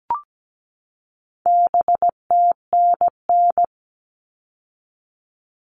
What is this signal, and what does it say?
Morse code, heard as a keyed tone, about 17 wpm, 700 Hz. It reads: BTNN
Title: All Cut Numbers - 4-Digit 17wpm